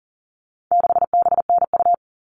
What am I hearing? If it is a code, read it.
Morse code, heard as a keyed tone, about 40 wpm, 700 Hz. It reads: 6BNU
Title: All Cut Numbers - 4-Digit 40wpm